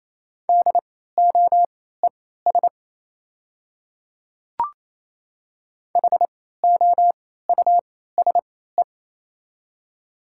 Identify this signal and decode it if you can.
Morse code, heard as a keyed tone, about 28 wpm, spaced as for 16 wpm, 690 Hz. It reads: DOES HOUSE
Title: 3x Character Spacing - Top 500 Words - 28wpm